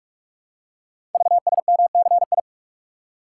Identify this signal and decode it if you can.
Morse code, heard as a keyed tone, about 44 wpm, 700 Hz. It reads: VSMCI
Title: Code Groups - 5 Letter:Numbers - 45wpm